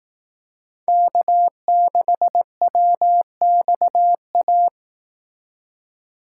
Morse code, heard as a keyed tone, about 18 wpm, 700 Hz. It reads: K6WXA
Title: Speed Racing - USA 1x3 Call Signs 15wpm